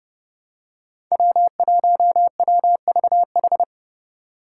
Morse code, heard as a keyed tone, about 30 wpm, 700 Hz. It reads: W1WVH
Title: Five Challenging Letter-Number Combinations 30wpm